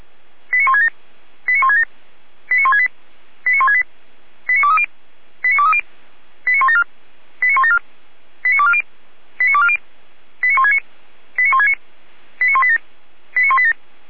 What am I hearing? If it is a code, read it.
Five-tone selective call (ZVEI), heard as quick runs of five tones, tones of about 70 ms. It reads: 89167 89167 89168 89168 89230 89230 89174 89174 89240 89240 89169 89169 89187 89187
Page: ZVEI